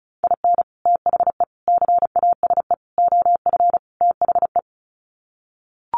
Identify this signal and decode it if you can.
Morse code, heard as a keyed tone, about 34 wpm, 700 Hz. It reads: IN THE CASE OF THE
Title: Sets of 5 Words 35wpm